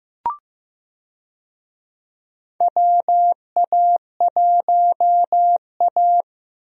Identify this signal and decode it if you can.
Morse code, heard as a keyed tone, about 15 wpm, 700 Hz. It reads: WA1A
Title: USA Call Signs - 4 Characters - 15wpm